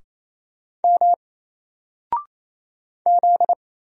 Morse code, heard as a keyed tone, about 28 wpm, 700 Hz. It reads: M Z
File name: Single-Letter-Number-28wpm